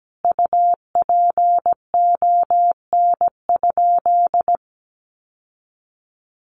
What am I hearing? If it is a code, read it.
Morse code, heard as a keyed tone, about 17 wpm, 700 Hz. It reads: UPON?
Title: Sentences from Top 1100 Words 17wpm